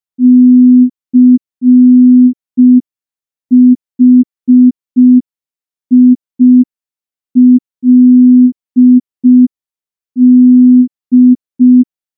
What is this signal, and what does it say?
Morse code, heard as a keyed tone, about 5 wpm, 250 Hz. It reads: CHILD
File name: morse / morse-training-05WPM